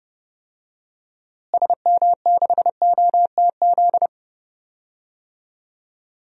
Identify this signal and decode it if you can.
Morse code, heard as a keyed tone, about 29 wpm, 700 Hz. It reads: SM6OTZ
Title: DXCC-100 Call Signs - 6 Characters - Repeated 2x - 30wpm